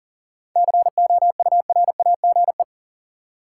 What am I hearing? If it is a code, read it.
Morse code, heard as a keyed tone, about 40 wpm, 700 Hz. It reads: COURAGE
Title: Seven Letter Words - 40wpm